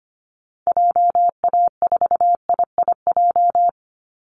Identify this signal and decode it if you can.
Morse code, heard as a keyed tone, about 25 wpm, 700 Hz. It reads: JA4IIJ
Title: Call Signs - 25wpm